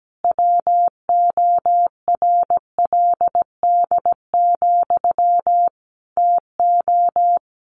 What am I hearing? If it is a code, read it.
Morse code, heard as a keyed tone, about 17 wpm, 700 Hz. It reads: WORLD, TO
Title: A Sound Of Thunder 17wpm